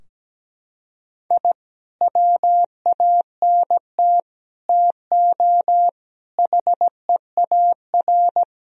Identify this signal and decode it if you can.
Morse code, heard as a keyed tone, about 17 wpm, 700 Hz. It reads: I WANT TO HEAR
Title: Sentences from Top 1000 Words - Review Part 4 - 17wpm